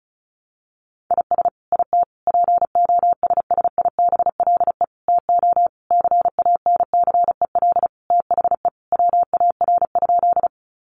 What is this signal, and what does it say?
Morse code, heard as a keyed tone, about 34 wpm, 700 Hz. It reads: IS IT POSSIBLE TO CANCEL THE WAR?
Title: Sentences from Top 1200 Words 35wpm